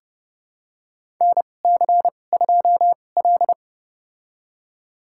Morse code, heard as a keyed tone, about 30 wpm, spaced as for 22 wpm, 700 Hz. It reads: NC2L
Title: USA Call Signs - 2x Character Spacing - 30wpm